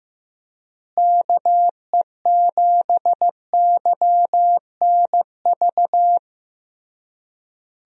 Morse code, heard as a keyed tone, about 15 wpm, 700 Hz. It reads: KE7YNV
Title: USA Call Signs - 6 Characters - Repeated 3x - 15wpm